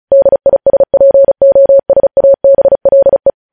Morse code, heard as a keyed tone, about 36 wpm, 550 Hz. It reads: DISPOSABLE
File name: podcast / News headlines-1x-35x35